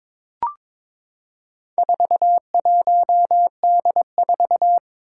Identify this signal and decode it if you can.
Morse code, heard as a keyed tone, about 21 wpm, 700 Hz. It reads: 41D4
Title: Four Challenging Letter-Number Combinations 22wpm